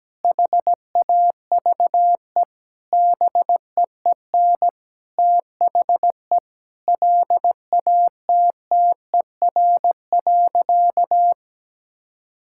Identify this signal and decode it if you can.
Morse code, heard as a keyed tone, about 17 wpm, 700 Hz. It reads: HAVE BEEN THE LATTER.
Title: Sentences from Top 1600 Words 17wpm